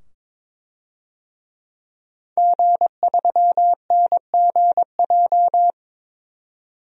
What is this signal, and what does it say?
Morse code, heard as a keyed tone, about 22 wpm, 700 Hz. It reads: G3NGJ
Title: DXCC-100 Call Signs - 5 Characters - 22wpm